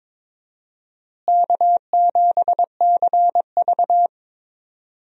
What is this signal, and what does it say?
Morse code, heard as a keyed tone, about 22 wpm, 700 Hz. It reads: K7CV
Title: USA Call Signs - 4 Characters - Repeated 3x - 22wpm